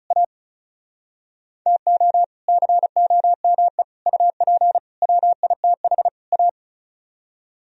Morse code, heard as a keyed tone, about 47 wpm, spaced as for 41 wpm, 700 Hz. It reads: A TO COME UP WITH A
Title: Speed-Racing - Sets of 5 Words 35wpm